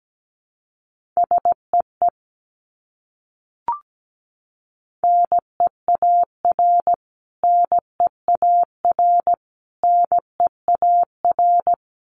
Morse code, heard as a keyed tone, about 17 wpm, 700 Hz. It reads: SEE NEAR NEAR NEAR
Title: Top 300 Words Repeated 17wpm